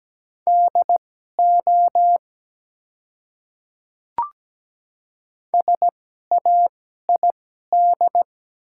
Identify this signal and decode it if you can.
Morse code, heard as a keyed tone, about 17 wpm, spaced as for 12 wpm, 700 Hz. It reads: DO SAID
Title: 2x Character Spacing - Top 100 Words - 17wpm